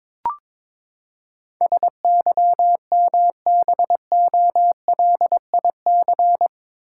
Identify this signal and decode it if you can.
Morse code, heard as a keyed tone, about 21 wpm, 700 Hz. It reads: SYMBOLIC
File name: practice / 22wpm / Eight Letter Words - 22wpm